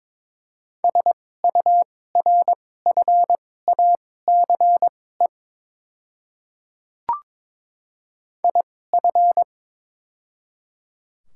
Morse code, heard as a keyed tone, about 22 wpm, spaced as for 16 wpm, 690 Hz. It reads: SURFACE IF
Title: 2x Character Spacing - Top 500 Words - 22wpm